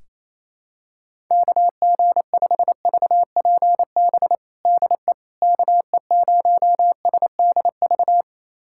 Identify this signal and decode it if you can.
Morse code, heard as a keyed tone, about 28 wpm, 700 Hz. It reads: KG5VPB DE KE0SDV
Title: USA Callsign DE Callsign 28wpm